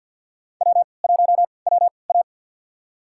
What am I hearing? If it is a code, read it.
Morse code, heard as a keyed tone, about 48 wpm, spaced as for 28 wpm, 700 Hz. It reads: W1WA
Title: USA Call Signs - 4 Characters - 3x Character Spacing - 50wpm